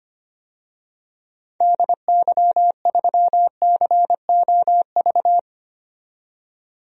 Morse code, heard as a keyed tone, about 25 wpm, 700 Hz. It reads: DY3COV
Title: DXCC-100 Call Signs - 6 Characters - 25wpm